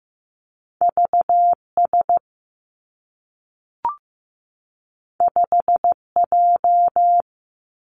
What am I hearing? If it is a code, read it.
Morse code, heard as a keyed tone, about 15 wpm, 700 Hz. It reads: V S 5 J
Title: Two Challenging Letter-Number Combinations 15wpm